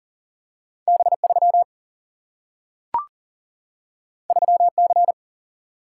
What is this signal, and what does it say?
Morse code, heard as a keyed tone, about 39 wpm, 700 Hz. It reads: B3 3C
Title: Single-Letter Single-Number 40wpm